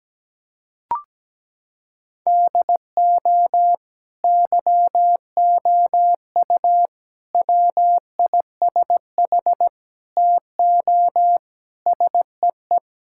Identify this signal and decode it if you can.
Morse code, heard as a keyed tone, about 17 wpm, 700 Hz. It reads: DO YOU WISH TO SEE
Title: Sentences from Top 1600 Words 17wpm